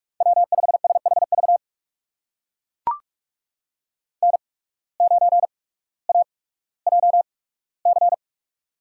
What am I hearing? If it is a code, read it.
Morse code, heard as a keyed tone, about 42 wpm, spaced as for 19 wpm, 700 Hz. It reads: W5SHV N9AJC
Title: USA Call Signs - 5 Characters - 8x Character Spacing - 45wpm